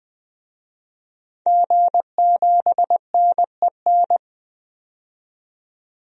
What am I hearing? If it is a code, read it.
Morse code, heard as a keyed tone, about 21 wpm, 700 Hz. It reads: G7NEN
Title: DXCC-100 Call Signs - 5 Characters - 20wpm